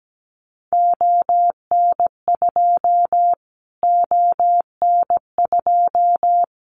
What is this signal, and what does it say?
Morse code, heard as a keyed tone, about 17 wpm, 700 Hz. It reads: ON2 ON2
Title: Callsign Prefixes for Top 25 DXCC - Repeated 3x - 17wpm